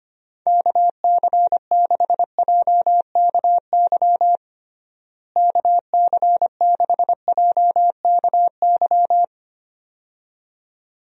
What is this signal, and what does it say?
Morse code, heard as a keyed tone, about 25 wpm, 700 Hz. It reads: KC6JKY KC6JKY
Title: USA Call Signs - 6 Characters - Repeated 2x - 25wpm